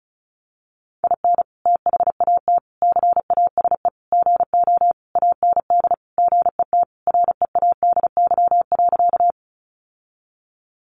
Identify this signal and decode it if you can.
Morse code, heard as a keyed tone, about 34 wpm, 700 Hz. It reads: IN THAT CASE GO AND GET READY.
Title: Sentences from Top 700 Words 35wpm